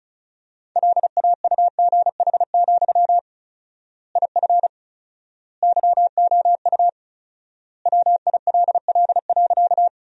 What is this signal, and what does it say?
Morse code, heard as a keyed tone, about 34 wpm, 700 Hz. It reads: LAUGH, IF YOU WILL.
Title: 4x Word-Spacing - Sentences from Top 500 Words - 35wpm